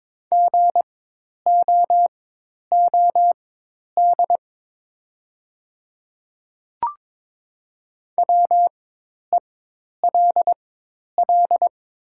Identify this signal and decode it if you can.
Morse code, heard as a keyed tone, about 22 wpm, spaced as for 10 wpm, 700 Hz. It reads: GOOD WELL
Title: 4x Character Spacing - Top 200 Words - 22wpm